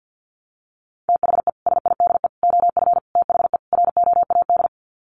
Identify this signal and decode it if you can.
Morse code, heard as a keyed tone, about 50 wpm, 700 Hz. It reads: THE SIDE OF THE ROAD
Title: Sets of 5 Words 50wpm